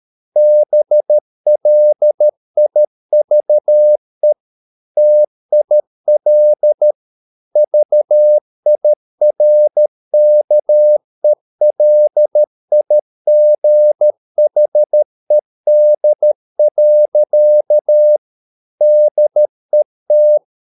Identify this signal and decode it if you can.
Morse code, heard as a keyed tone, about 13 wpm, 600 Hz. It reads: BLIVE TIL VIRKELIGHED. DET
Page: Ferie 13 wpm